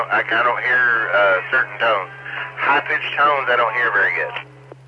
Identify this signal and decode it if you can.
Emergency Alert System (SAME) tones, heard as harsh buzzing bursts, with speech. It is heard from 0.6 to 4.2 s.